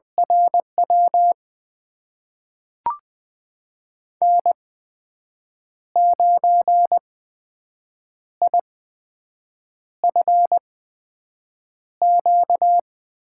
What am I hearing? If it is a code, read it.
Morse code, heard as a keyed tone, about 20 wpm, spaced as for 6 wpm, 700 Hz. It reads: RW N9IFQ
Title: USA Call Signs - 5 Characters - 8x Character Spacing - 20wpm